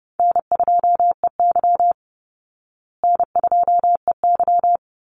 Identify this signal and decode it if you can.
Morse code, heard as a keyed tone, about 30 wpm, 700 Hz. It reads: N2EY N2EY
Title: Easy Call Signs Repeated 3x - 30wpm